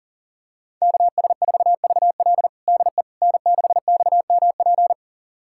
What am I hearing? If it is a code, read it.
Morse code, heard as a keyed tone, about 40 wpm, 700 Hz. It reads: KS4VL DE N6XMP
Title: USA Callsign DE Callsign - 5 Characters - 40wpm